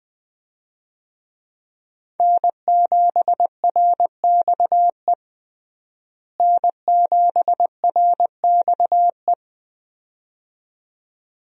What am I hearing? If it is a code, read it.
Morse code, heard as a keyed tone, about 21 wpm, 700 Hz. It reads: N7RXE N7RXE
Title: USA Call Signs - 5 Characters - Repeated 2x - 20wpm